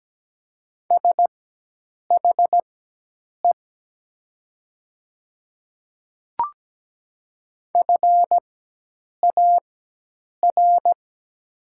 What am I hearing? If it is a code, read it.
Morse code, heard as a keyed tone, about 17 wpm, spaced as for 8 wpm, 700 Hz. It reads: SHE FAR
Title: 4x Character Spacing - Top 300 Words - 17wpm